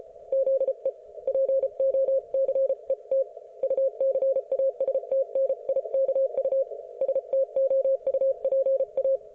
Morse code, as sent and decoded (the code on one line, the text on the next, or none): --.. / .--. --- -.-. - / ..- -.-. .- ... - -. .. -.- ..- / ... - --- ..- .--. .-
ZE POCET UCASTNIKU STOUPA